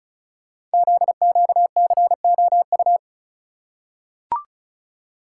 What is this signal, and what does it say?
Morse code, heard as a keyed tone, about 36 wpm, 700 Hz. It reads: ZQCOU
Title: Code Groups - 5 Letters - 35wpm